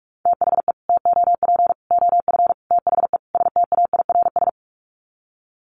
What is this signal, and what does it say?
Morse code, heard as a keyed tone, about 44 wpm, 700 Hz. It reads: THE TOP OF THE STAIRS
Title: Sets of 5 Words Encore 45wpm